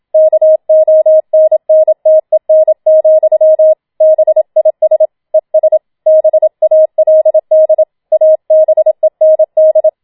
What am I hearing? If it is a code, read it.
Morse code, heard as a keyed tone, about 27 wpm, 630 Hz. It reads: KONNTEN, BIS ES BALD ABEND
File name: Schneewittchen Teil2 27wpm